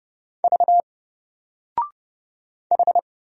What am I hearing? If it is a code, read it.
Morse code, heard as a keyed tone, about 30 wpm, 700 Hz. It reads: V H
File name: Single-Letter-Number-30wpm